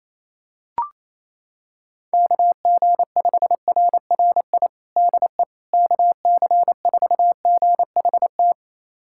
Morse code, heard as a keyed tone, about 28 wpm, 700 Hz. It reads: KG5RRI DE KC4GHT
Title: USA Callsign DE Callsign - 6 Characters - 28wpm